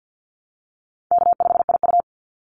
Morse code, heard as a keyed tone, about 50 wpm, 700 Hz. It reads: K5IU
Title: Call Signs - 50wpm